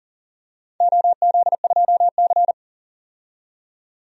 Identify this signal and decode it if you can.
Morse code, heard as a keyed tone, about 40 wpm, 700 Hz. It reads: OZ2C
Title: DXCC-100 Call Signs - 4 Characters - 40wpm